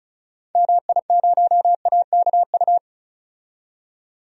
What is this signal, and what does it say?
Morse code, heard as a keyed tone, about 34 wpm, 700 Hz. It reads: MI0AKU
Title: DXCC-100 Call Signs - 6 Characters - 35wpm